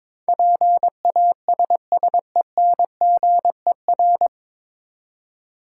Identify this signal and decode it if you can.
Morse code, heard as a keyed tone, about 22 wpm, 700 Hz. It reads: PASSENGER
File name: 9 Letter Words - First 3 Letters Given - 22wpm